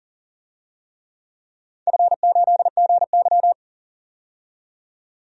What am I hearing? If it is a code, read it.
Morse code, heard as a keyed tone, about 40 wpm, 700 Hz. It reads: F8GY
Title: DXCC-100 Call Signs - 4 Characters - 40wpm